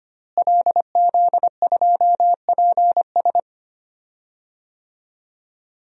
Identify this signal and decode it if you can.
Morse code, heard as a keyed tone, about 25 wpm, 700 Hz. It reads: LZ2PS
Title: DXCC-100 Call Signs - 5 Characters - Repeated 2x - 25wpm